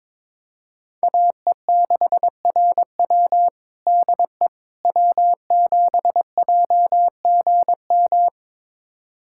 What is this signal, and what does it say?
Morse code, heard as a keyed tone, about 22 wpm, 700 Hz. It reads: AE6RW DE W7JGM
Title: USA Callsign DE Callsign - 5 Characters - 22wpm